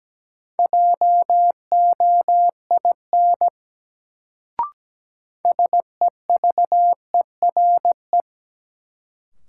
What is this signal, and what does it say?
Morse code, heard as a keyed tone, about 17 wpm, 700 Hz. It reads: JOIN SEVERE
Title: Sentences from Top 1400 Words - New Word Drill Alt - 17wpm